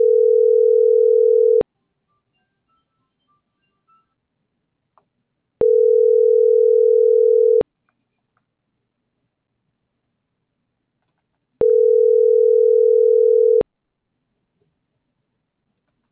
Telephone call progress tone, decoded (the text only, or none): ringback tone